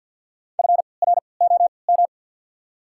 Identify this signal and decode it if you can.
Morse code, heard as a keyed tone, about 52 wpm, spaced as for 29 wpm, 700 Hz. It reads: FROM
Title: 3x Character Spacing - Top 200 Words - 50wpm